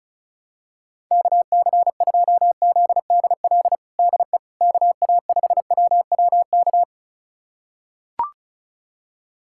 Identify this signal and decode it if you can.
Morse code, heard as a keyed tone, about 36 wpm, 700 Hz. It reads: KC2ZDL DE KA5WWK
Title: USA Callsign DE Callsign 35wpm